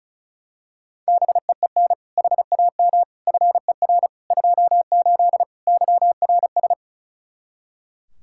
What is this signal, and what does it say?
Morse code, heard as a keyed tone, about 34 wpm, 700 Hz. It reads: BEEN HAM FER 28 YRS
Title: Practice QSO - Exchange 3 - More Info - 35wpm